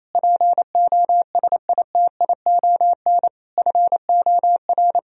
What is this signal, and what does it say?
Morse code, heard as a keyed tone, about 28 wpm, 700 Hz. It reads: POSITION FOR
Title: Sentences from Top 1000 Words - Review Part 4 - 28wpm